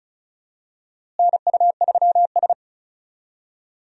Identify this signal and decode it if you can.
Morse code, heard as a keyed tone, about 34 wpm, 700 Hz. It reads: NU3S
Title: USA Call Signs - 4 Characters - Repeated 3x - 35wpm